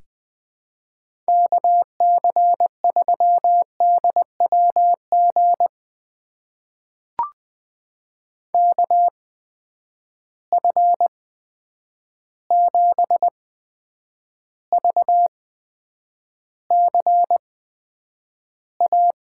Morse code, heard as a keyed tone, about 20 wpm, spaced as for 9 wpm, 700 Hz. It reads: KC3DWG KF7VCA
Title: USA Call Signs - 8x Character Spacing - 20wpm